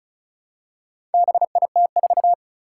Morse code, heard as a keyed tone, about 36 wpm, 700 Hz. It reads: BIT4
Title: Four Challenging Letter-Number Combinations 35wpm